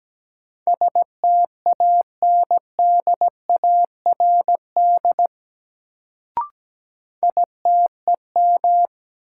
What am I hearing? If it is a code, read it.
Morse code, heard as a keyed tone, about 17 wpm, 700 Hz. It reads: STANDARD ITEM
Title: Sentences from Top 1000 Words - Review Part 6 - New Word Drill - 17wpm